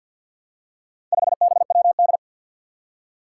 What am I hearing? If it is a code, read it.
Morse code, heard as a keyed tone, about 50 wpm, 700 Hz. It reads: 5BWD
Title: Four Challenging Letter-Number Combinations 50wpm